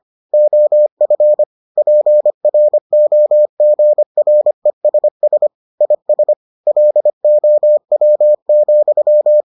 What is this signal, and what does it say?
Morse code, heard as a keyed tone, about 25 wpm, 600 Hz. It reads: OF PROGRESS IS LOW,